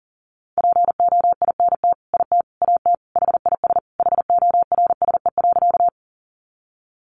Morse code, heard as a keyed tone, about 40 wpm, 700 Hz. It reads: POINT IT AT HIS HORSE.
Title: Sentences from Top 300 Words 40wpm